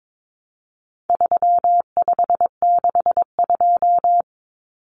Morse code, heard as a keyed tone, about 22 wpm, 700 Hz. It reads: 3562
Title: Numbers - 4-Digit 22wpm